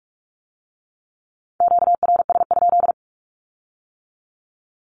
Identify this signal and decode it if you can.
Morse code, heard as a keyed tone, about 46 wpm, 700 Hz. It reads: QRS?
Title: Q-Codes with Spoken Meaning - 45wpm